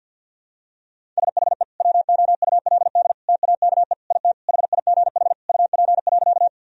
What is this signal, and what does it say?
Morse code, heard as a keyed tone, about 52 wpm, 700 Hz. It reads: SHE WOULD TAKE IT HIGH UP.
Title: Sentences from Top 1000 Words - Review Part 1 - 50wpm